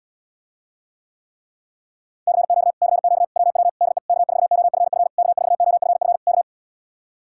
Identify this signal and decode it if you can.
Morse code, heard as a keyed tone, about 80 wpm, 700 Hz. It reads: CQ CQ WW DE K4OFF K4OFF K